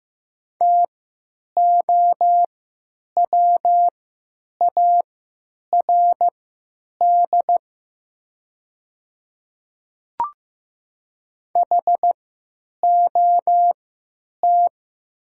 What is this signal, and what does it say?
Morse code, heard as a keyed tone, about 15 wpm, spaced as for 9 wpm, 700 Hz. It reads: TOWARD HOT
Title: 3x Character Spacing - Top 500 Words - 15wpm